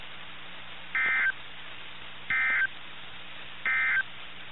Emergency Alert System (SAME) tones, heard as harsh buzzing bursts. It is heard from 0.8 to 4.1 s.
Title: NOAA Weather Radio - KIH61